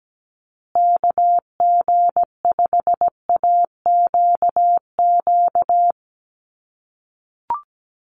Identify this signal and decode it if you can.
Morse code, heard as a keyed tone, about 17 wpm, 700 Hz. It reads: KG5AQQ
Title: USA Call Signs Repeated 3x 17wpm